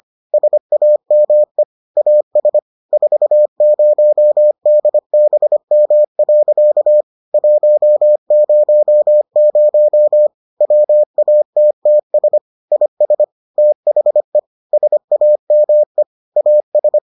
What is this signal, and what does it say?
Morse code, heard as a keyed tone, about 25 wpm, 600 Hz. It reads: SAME AS 40DBM. 100 WATTS IS THE SAME AS